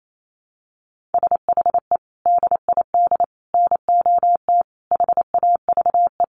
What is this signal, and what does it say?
Morse code, heard as a keyed tone, about 28 wpm, 700 Hz. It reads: SHE DID NOT HAVE
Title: Sets of 5 Words Encore2 28wpm